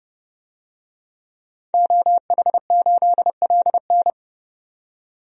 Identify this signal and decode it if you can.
Morse code, heard as a keyed tone, about 30 wpm, 690 Hz. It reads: OH8LN
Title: DXCC-100 Call Signs - 5 Characters - 30wpm